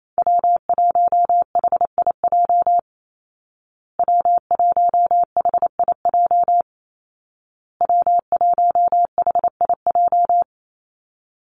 Morse code, heard as a keyed tone, about 28 wpm, 700 Hz. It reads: W1HIJ W1HIJ W1HIJ
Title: Call Signs Repeated 2x - 28wpm